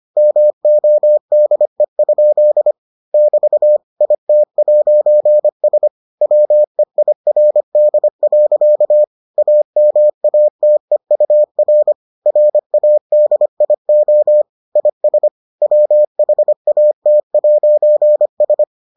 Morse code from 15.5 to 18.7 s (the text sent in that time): WHAT'S